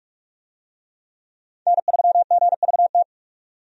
Morse code, heard as a keyed tone, about 44 wpm, 700 Hz. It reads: N3GVT
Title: USA Call Signs - 5 Characters - 45wpm